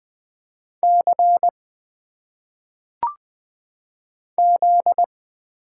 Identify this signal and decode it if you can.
Morse code, heard as a keyed tone, about 20 wpm, 700 Hz. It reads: C Z
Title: Single Letters - Phonetic Pronunciation 20wpm